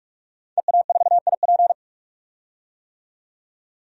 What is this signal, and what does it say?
Morse code, heard as a keyed tone, about 46 wpm, 700 Hz. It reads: EA4IP
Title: DXCC-100 Call Signs - 5 Characters - 45wpm